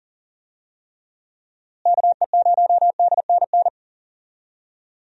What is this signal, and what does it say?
Morse code, heard as a keyed tone, about 40 wpm, 700 Hz. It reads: KE0DNN
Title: USA Call Signs - 6 Characters - Repeated 2x - 40wpm